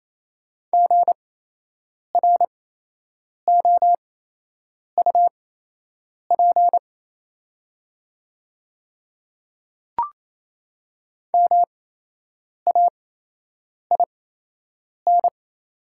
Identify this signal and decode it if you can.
Morse code, heard as a keyed tone, about 28 wpm, spaced as for 8 wpm, 700 Hz. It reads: GROUP MAIN
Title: 8x Character Spacing - Top 300 Words - 28wpm